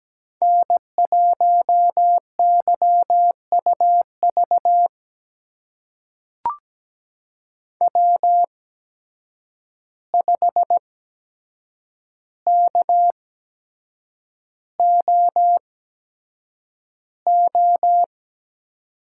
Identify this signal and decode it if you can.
Morse code, heard as a keyed tone, about 17 wpm, spaced as for 7 wpm, 700 Hz. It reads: N1YUV W5KOO
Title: USA Call Signs - 5 Characters - 8x Character Spacing - 17wpm